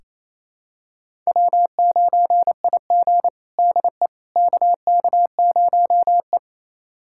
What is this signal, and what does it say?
Morse code, heard as a keyed tone, about 28 wpm, 700 Hz. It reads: W9IG DE KK0E
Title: USA Callsign DE Callsign - 4 Characters - 28wpm